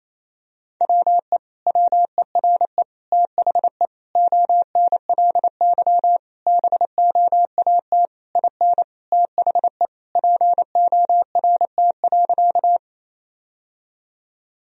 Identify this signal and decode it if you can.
Morse code, heard as a keyed tone, about 28 wpm, 700 Hz. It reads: WE WERE THE ONLY BOAT IN THE PORT.
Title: Sentences from Top 1000 Words - Review Part 1 - 28wpm